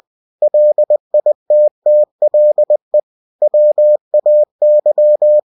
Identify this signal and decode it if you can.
Morse code, heard as a keyed tone, about 20 wpm, 600 Hz. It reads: LITTLE WAY
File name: cw-books / winnie-the-pooh / winnie-20wpm-page-0057